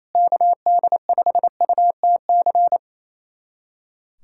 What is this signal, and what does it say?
Morse code, heard as a keyed tone, about 28 wpm, 700 Hz. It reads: KD5UTC
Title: USA Callsign DE Callsign 28wpm